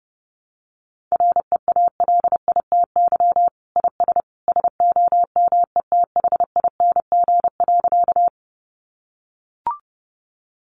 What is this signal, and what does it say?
Morse code, heard as a keyed tone, about 30 wpm, 700 Hz. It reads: REALITY IS SOMETHING.
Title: Sentences from Top 1200 Words 30wpm